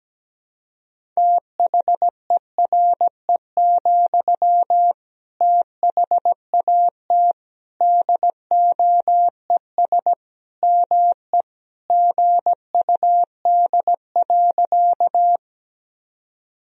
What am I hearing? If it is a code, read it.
Morse code, heard as a keyed tone, about 17 wpm, 700 Hz. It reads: THERE, THAT DOES ME GUD.
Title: Sentences from Top 200 Words and Abbreviations 17wpm